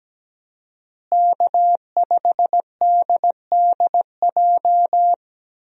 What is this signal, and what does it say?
Morse code, heard as a keyed tone, about 17 wpm, 700 Hz. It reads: K5DDJ
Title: USA Call Signs - 5 Characters - Repeated 2x - 17wpm